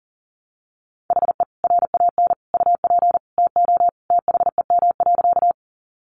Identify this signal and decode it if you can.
Morse code, heard as a keyed tone, about 40 wpm, 700 Hz. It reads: HE RAN UP TO THEM.
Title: Sentences from Top 500 Words 40wpm